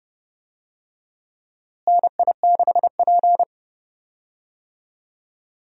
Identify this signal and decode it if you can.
Morse code, heard as a keyed tone, about 31 wpm, 700 Hz. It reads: NI6P
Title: USA Call Signs - 4 Characters - Repeated 2x - 30wpm